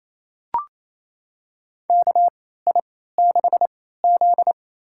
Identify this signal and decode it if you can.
Morse code, heard as a keyed tone, about 28 wpm, spaced as for 16 wpm, 700 Hz. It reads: KI6Z
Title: USA Call Signs - 6 Characters - 3x Character Spacing - 28wpm